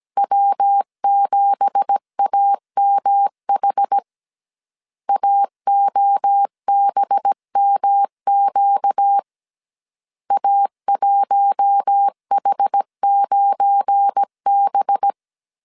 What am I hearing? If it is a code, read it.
Morse code, heard as a keyed tone, about 17 wpm, 800 Hz. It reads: W7AMH AOBMQ A1H9B
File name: CW / DET / 1 9 1500C S C 16wpm 2909